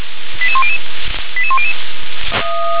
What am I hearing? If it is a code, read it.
Five-tone selective call (ZVEI), heard as quick runs of five tones, tones of about 70 ms. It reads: 88100 88100